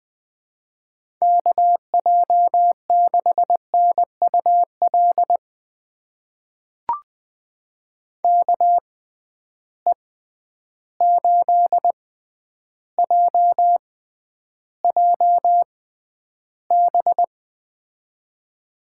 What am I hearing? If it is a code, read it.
Morse code, heard as a keyed tone, about 21 wpm, spaced as for 10 wpm, 700 Hz. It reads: KJ6NUL KE8JJB
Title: USA Call Signs - 6 Characters - 6x Character Spacing - 20wpm